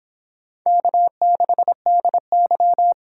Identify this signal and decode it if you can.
Morse code, heard as a keyed tone, about 26 wpm, 700 Hz. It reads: K6DY